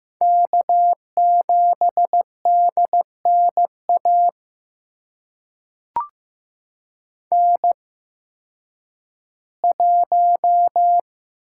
Morse code, heard as a keyed tone, about 16 wpm, spaced as for 10 wpm, 700 Hz. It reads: K7DNA N1
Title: USA Call Signs - 5 Characters - 8x Character Spacing - 15wpm